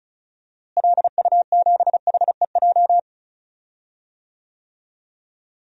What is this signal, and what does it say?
Morse code, heard as a keyed tone, about 34 wpm, 700 Hz. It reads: LU7HEJ
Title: DXCC-100 Call Signs - 6 Characters - Repeated 3x - 35wpm